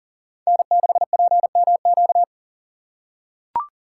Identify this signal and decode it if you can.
Morse code, heard as a keyed tone, about 40 wpm, 700 Hz. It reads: N6PMQ
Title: USA Call Signs - 5 Characters - 3x Character Spacing - 40wpm